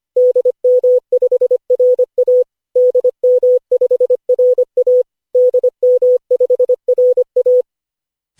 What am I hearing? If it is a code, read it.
Morse code, heard as a keyed tone, about 25 wpm, 500 Hz. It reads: DM5RA DM5RA DM5RA
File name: cw / Session5 cw training